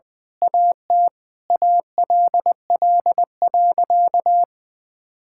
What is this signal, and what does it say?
Morse code, heard as a keyed tone, about 20 wpm, 700 Hz. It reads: AT ALL.
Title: Sentences from Top 1600 Words 20wpm